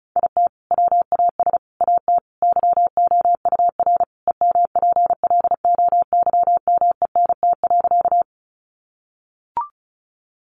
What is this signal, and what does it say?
Morse code, heard as a keyed tone, about 34 wpm, 700 Hz. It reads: IT WAS AT YOUR EMPLOYMENT.
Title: Sentences from Top 1500 Words 35wpm